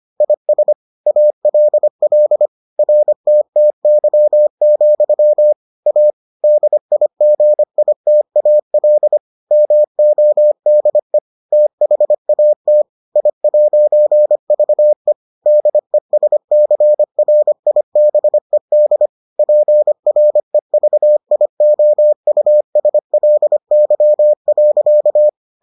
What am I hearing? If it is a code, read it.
Morse code, heard as a keyed tone, about 25 wpm, 600 Hz. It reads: IS ALL RTTY, A DIGITAL MODE THAT I'VE DESCRIBED PREVIOUSLY.